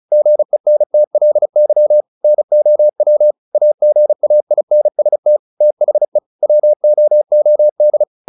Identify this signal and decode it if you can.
Morse code, heard as a keyed tone, about 34 wpm, 600 Hz. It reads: GENTLY NOW AGAINST THE WOOD
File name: the old man and the sea / the old man and the sea 1 35wpm